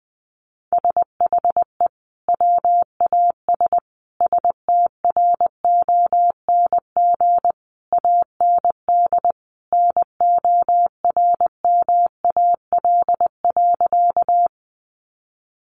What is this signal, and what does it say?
Morse code, heard as a keyed tone, about 20 wpm, 700 Hz. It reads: SHE WAS STRONG AND NORMAL.
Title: Sentences from Top 1200 Words 20wpm